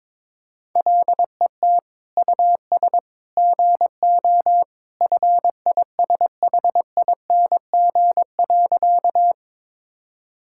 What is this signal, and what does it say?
Morse code, heard as a keyed tone, about 22 wpm, 700 Hz. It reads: LET US GO FISHING.
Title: Sentences from Top 1000 Words - Review Part 5 - 22wpm